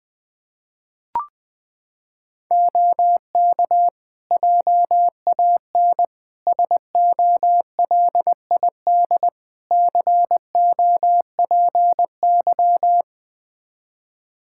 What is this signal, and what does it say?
Morse code, heard as a keyed tone, about 20 wpm, 700 Hz. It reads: OK JAN SOLID COPY
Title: K1USN SST QSO Practice 20wpm